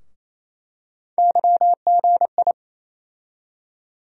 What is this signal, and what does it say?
Morse code, heard as a keyed tone, about 28 wpm, 700 Hz. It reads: YGI
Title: USA Call Sign Postfixes - 3 Letters 28wpm